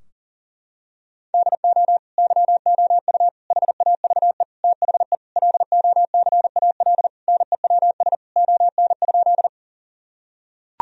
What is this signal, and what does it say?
Morse code, heard as a keyed tone, about 40 wpm, 700 Hz. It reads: DO YOU HAVE THE LOCAL NEWS ON?
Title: Sentences from Top 1000 Words - Review Part 3 - 40wpm